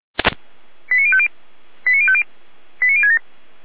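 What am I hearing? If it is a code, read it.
Five-tone selective call (ZVEI), heard as quick runs of five tones, tones of about 70 ms. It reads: 89050 89050 89067
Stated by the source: ZVEI five-tone selective call received on an SDR.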